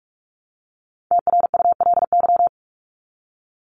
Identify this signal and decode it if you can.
Morse code, heard as a keyed tone, about 44 wpm, 700 Hz. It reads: TRULY
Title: LY Suffix - Top 100 Words with Base Words 45wpm